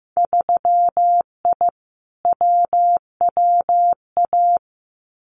Morse code, heard as a keyed tone, about 15 wpm, 700 Hz. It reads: 3I WWA
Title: Winter Field Day Exchanges Encore 15wpm